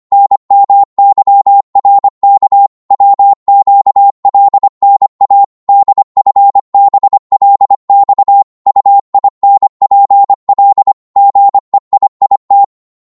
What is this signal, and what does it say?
Morse code, heard as a keyed tone, about 25 wpm, 800 Hz. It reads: NMYRK WQLNA DFBLX UINPL GEIIT